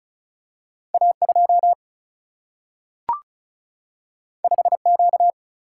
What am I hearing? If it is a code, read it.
Morse code, heard as a keyed tone, about 36 wpm, 700 Hz. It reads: A 2 5 Q
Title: Single-Letter Single-Number 35wpm